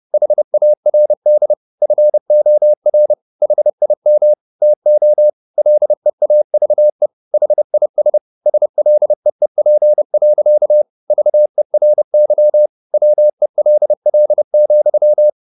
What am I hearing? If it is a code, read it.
Morse code, heard as a keyed tone, about 30 wpm, 600 Hz. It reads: HARD FOR HIM TO LEAVE HIS SLEEP. VERY WELL,
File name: the old man and the sea / the old man and the sea 1 30wpm